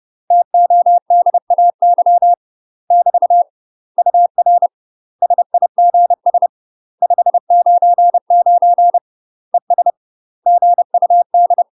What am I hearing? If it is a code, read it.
Morse code, heard as a keyed tone, about 30 wpm, 700 Hz. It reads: TODAY <BT> UR SIGS 599 ES GUD
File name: UK MASTER QSOs TRAINING / 30WPM / UK MASTER QSOs 0018